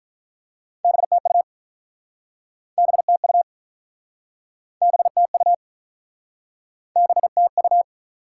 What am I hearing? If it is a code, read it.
Morse code, heard as a keyed tone, about 48 wpm, 700 Hz. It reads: BTU BTU BTU BTU